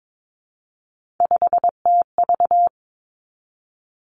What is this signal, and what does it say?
Morse code, heard as a keyed tone, about 22 wpm, 700 Hz. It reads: 5TV
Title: Three Challenging Letter-Number Combinations 22wpm